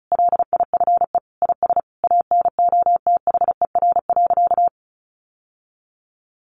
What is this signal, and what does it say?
Morse code, heard as a keyed tone, about 34 wpm, 700 Hz. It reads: LIFE IS ANOTHER.
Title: Sentences from Top 1000 Words 35wpm